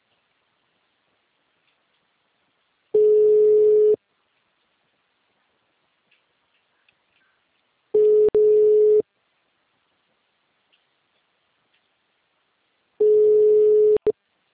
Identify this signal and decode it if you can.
Ringback tone (Europe, 425 Hz).
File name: out- -2032